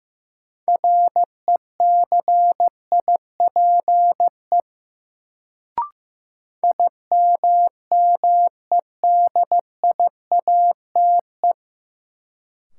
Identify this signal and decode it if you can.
Morse code, heard as a keyed tone, about 15 wpm, 700 Hz. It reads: RECIPE IMMEDIATE
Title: Sentences from Top 1200 Words - New Word Drill - 15wpm